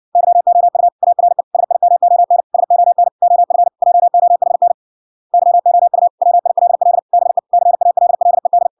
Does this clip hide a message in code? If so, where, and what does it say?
Yes: Morse code from 0.2 to 8.7 s, heard as a keyed tone, about 57 wpm, 700 Hz. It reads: YOU ARE SIMON SON OF JOHN YOU WILL BE CALLED